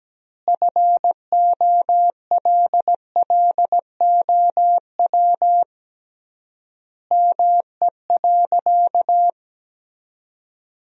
Morse code, heard as a keyed tone, about 17 wpm, 700 Hz. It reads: FOLLOW ME.
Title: 3x Word-Spacing - Sentences from Top 300 Words - 17wpm